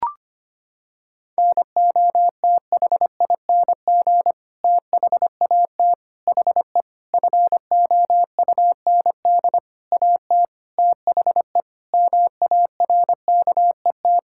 Morse code, heard as a keyed tone, about 25 wpm, 700 Hz. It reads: NOTHING THAT HE FOUND AT THE MARKET
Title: Sentences from Top 1000 Words - Review Part 6 - 25wpm